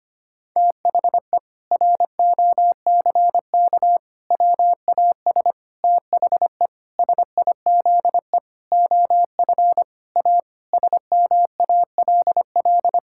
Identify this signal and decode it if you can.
Morse code, heard as a keyed tone, about 25 wpm, 700 Hz. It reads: THE ROCK WAS THE SIZE OF A SMALL
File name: practice / 25wpm / Sentences with 4-Word Phrases - 25wpm